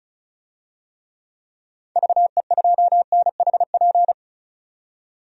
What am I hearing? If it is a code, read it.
Morse code, heard as a keyed tone, about 36 wpm, 700 Hz. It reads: VE2NHP
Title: DXCC-100 Call Signs - 6 Characters - 35wpm